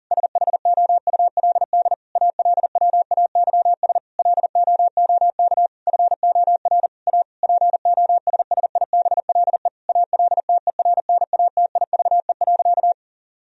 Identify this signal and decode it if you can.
Morse code, heard as a keyed tone, about 37 wpm, 700 Hz. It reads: SHOULD ALWAYS LOOK FOR A POSSIBLE ALTERNATIVE.
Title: Sentences from Top 1000 Words - Review Part 4 - 40wpm